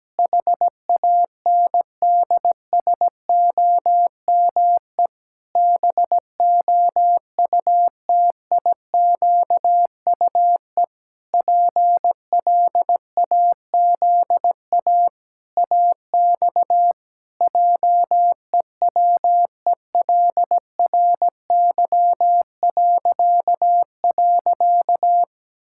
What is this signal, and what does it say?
Morse code, heard as a keyed tone, about 17 wpm, 700 Hz. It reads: HANDSOME BOUTIQUE PLAZA AX JEWELRY..
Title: Pangram Challenge v2 - 17wpm